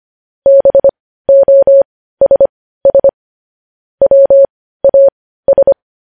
Morse code, heard as a keyed tone, about 25 wpm, spaced as for 15 wpm, 550 Hz. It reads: BOSS WAS
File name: podcast / News headlines-1x-25x15